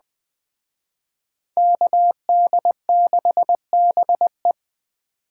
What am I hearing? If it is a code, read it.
Morse code, heard as a keyed tone, about 20 wpm, 700 Hz. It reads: KD6BE
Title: Speed Racing - USA 2x2 Call Signs 17wpm